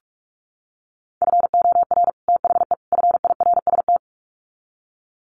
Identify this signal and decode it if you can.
Morse code, heard as a keyed tone, about 44 wpm, 700 Hz. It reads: FOR THE FIRST
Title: Sets of 3 Words 45wpm